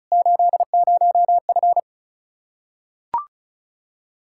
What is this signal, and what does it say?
Morse code, heard as a keyed tone, about 36 wpm, 700 Hz. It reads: 80F
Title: K1USN SST QSO Practice 35wpm